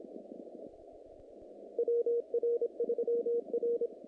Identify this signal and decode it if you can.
Morse code, heard as a keyed tone, about 26 wpm, 470 Hz. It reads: WR3R
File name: contests / naqp-cw-k5tr-op-k5pi- -2100